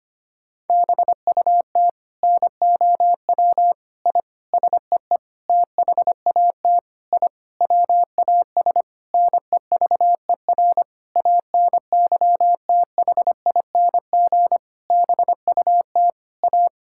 Morse code, heard as a keyed tone, about 25 wpm, 700 Hz. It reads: BUT NOW I SEE THAT I WAS NEVER ANYTHING BUT A
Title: Sentences from Top 1000 Words - Review Part 6 - 25wpm